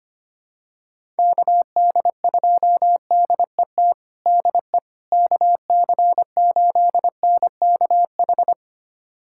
Morse code, heard as a keyed tone, about 25 wpm, 700 Hz. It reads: KD2DET DE KC8NKH
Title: USA Callsign DE Callsign - 6 Characters - 25wpm